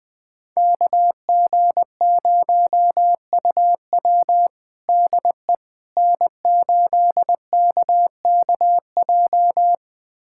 Morse code, heard as a keyed tone, about 20 wpm, 700 Hz. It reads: KG0UW DE N8KKJ
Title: USA Callsign DE Callsign - 5 Characters - 20wpm